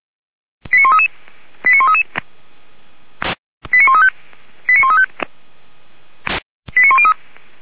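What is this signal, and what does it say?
Five-tone selective call (ZVEI), heard as quick runs of five tones, tones of about 70 ms. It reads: 89133 89133 89135 89135 89103
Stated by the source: ZVEI five-tone call received by SDR, feuerwehr-orlamuende.org fire brigade.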